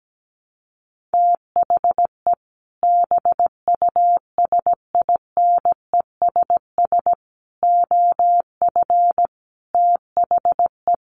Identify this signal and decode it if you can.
Morse code, heard as a keyed tone, about 17 wpm, 700 Hz. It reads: THE BUSINESS OF THE
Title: Sentences from Top 1500 Words 17wpm